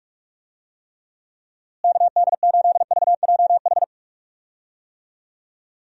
Morse code, heard as a keyed tone, about 46 wpm, 700 Hz. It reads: KD8VJH
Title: USA Call Signs - 6 Characters - Repeated 2x - 45wpm